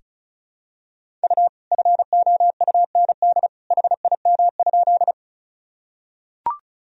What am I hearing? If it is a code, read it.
Morse code, heard as a keyed tone, about 36 wpm, 700 Hz. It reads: U FOUND HIM?
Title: Sentences from Top 300 Words and Abbreviations 35wpm